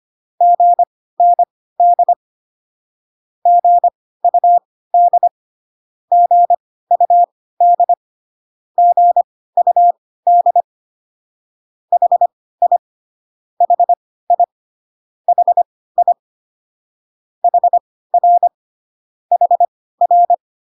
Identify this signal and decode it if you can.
Morse code, heard as a keyed tone, about 25 wpm, spaced as for 16 wpm, 700 Hz. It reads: GND GUD GUD GUD HI HI HI HR HR